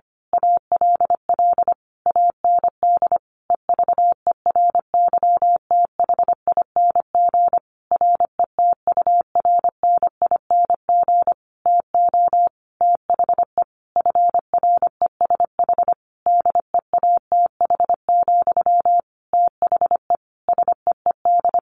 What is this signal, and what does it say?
Morse code, heard as a keyed tone, about 25 wpm, 700 Hz. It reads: ALL AND EVERYTHING RETURNING TO THE FRESH DEATH, THE SEED
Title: A Sound Of Thunder 25wpm